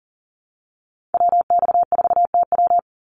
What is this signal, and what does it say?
Morse code, heard as a keyed tone, about 40 wpm, 700 Hz. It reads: WX4TW
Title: NA SOTA Activator Callsigns 40wpm